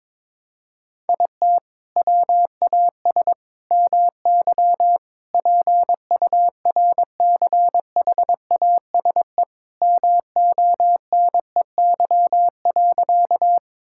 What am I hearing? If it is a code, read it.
Morse code, heard as a keyed tone, about 22 wpm, 700 Hz. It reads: IT WAS MY PURCHASE MONEY.
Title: Sentences from Top 1000 Words - Review Part 5 - 22wpm